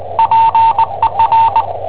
Morse code, heard as a keyed tone, about 20 wpm, 900 Hz, with radio noise. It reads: PER
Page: Cassetta esercizi preparata da I7OHP per IW5ELP